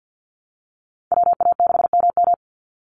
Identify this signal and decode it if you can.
Morse code, heard as a keyed tone, about 48 wpm, 700 Hz. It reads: WA6MM
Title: NA SOTA Activator Callsigns 50wpm